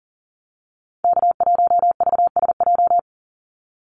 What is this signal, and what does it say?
Morse code, heard as a keyed tone, about 40 wpm, 700 Hz. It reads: K1VSJ
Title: Call Signs - 40wpm